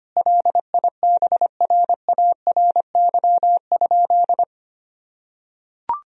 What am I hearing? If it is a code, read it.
Morse code, heard as a keyed tone, about 25 wpm, 700 Hz. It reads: LIBRARY?
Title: Sentences from Top 1000 Words - Review Part 4 - 25wpm